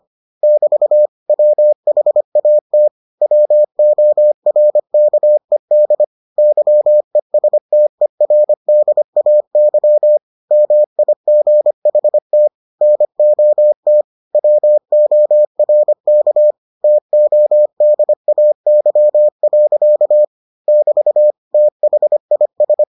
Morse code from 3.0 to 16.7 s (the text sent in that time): WORKED YESTERDAY MIGHT NOT WORK